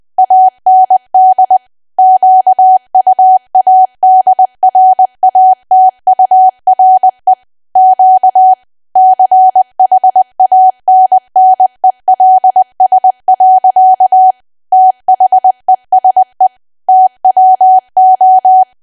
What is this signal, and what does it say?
Morse code, heard as a keyed tone, about 20 wpm, 750 Hz. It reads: AND QUADRATURE Q CHANNELS. THESE TWO
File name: cwa / W1AW-Level3 / 20WPM